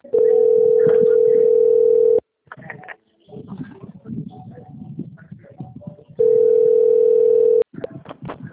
Ringback tone (North America), 440 + 480 Hz.
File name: out- - - - - .55804